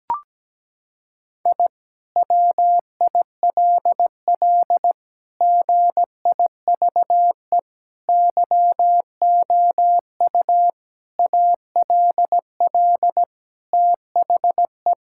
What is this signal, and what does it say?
Morse code, heard as a keyed tone, about 17 wpm, 700 Hz. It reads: I WILL GIVE YOU ALL THE
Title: Sentences from Top 1000 Words - Review Part 6 - 17wpm